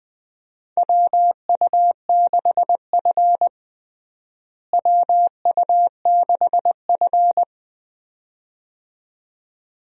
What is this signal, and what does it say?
Morse code, heard as a keyed tone, about 21 wpm, 700 Hz. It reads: WU6F WU6F
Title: USA Call Signs - 4 Characters - Repeated 2x - 20wpm